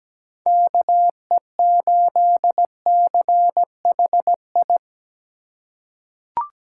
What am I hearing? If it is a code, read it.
Morse code, heard as a keyed tone, about 17 wpm, 700 Hz. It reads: KE8CHI
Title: USA Call Signs - 4x Character Spacing - 17wpm